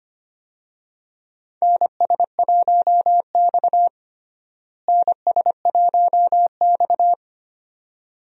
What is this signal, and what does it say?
Morse code, heard as a keyed tone, about 25 wpm, 700 Hz. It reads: NS1X NS1X
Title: USA Call Signs - 4 Characters - Repeated 2x - 25wpm